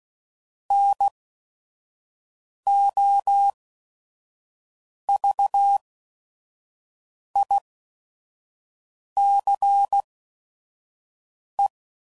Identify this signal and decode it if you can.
Morse code, heard as a keyed tone, about 16 wpm, spaced as for 5 wpm, 800 Hz. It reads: NOVICE
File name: CW Code Groups @5WPM / fromSM's / Classes